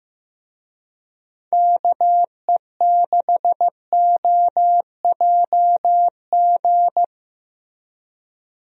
Morse code, heard as a keyed tone, about 15 wpm, 700 Hz. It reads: KE6OJG
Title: USA Call Signs - 6 Characters - Repeated 3x - 15wpm